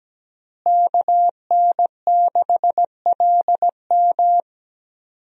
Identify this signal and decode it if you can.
Morse code, heard as a keyed tone, about 17 wpm, 700 Hz. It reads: KN6LM
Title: USA Call Signs - 5 Characters - Repeated 3x - 17wpm